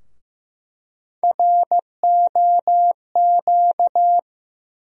Morse code, heard as a keyed tone, about 15 wpm, 700 Hz. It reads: ROQ
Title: USA Call Sign Postfixes - 3 Letters 15wpm